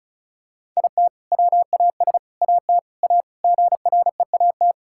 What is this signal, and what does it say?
Morse code, heard as a keyed tone, about 36 wpm, 700 Hz. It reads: IT WAS AT A GREAT
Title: Sentences from Top 1600 Words 35wpm